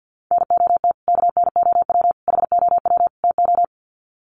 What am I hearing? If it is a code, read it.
Morse code, heard as a keyed tone, about 50 wpm, 700 Hz. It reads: NOT KNOW HOW TO
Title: Sets of 4 Words 50wpm